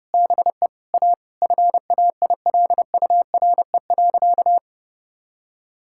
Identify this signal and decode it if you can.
Morse code, heard as a keyed tone, about 30 wpm, 700 Hz. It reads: BE A FAILURE.
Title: Sentences from Top 1000 Words - Review Part 6 - 30wpm